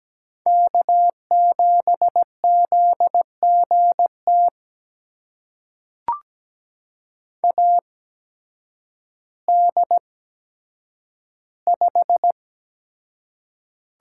Morse code, heard as a keyed tone, about 17 wpm, spaced as for 9 wpm, 700 Hz. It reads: K7ZGT AD5
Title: USA Call Signs - 8x Character Spacing - 17wpm